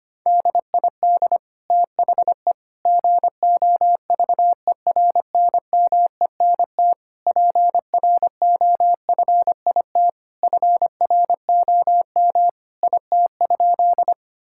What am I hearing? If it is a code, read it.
Morse code, heard as a keyed tone, about 25 wpm, 700 Hz. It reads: DID THE GOVERNMENT PROFIT FROM IT?
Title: Sentences from Top 1000 Words - Review Part 5 - 25wpm